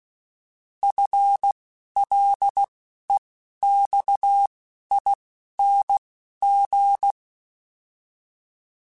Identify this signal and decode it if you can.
Morse code, heard as a keyed tone, about 16 wpm, spaced as for 12 wpm, 800 Hz. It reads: FLEXING
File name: FL / FL mp3s / Fast FL @10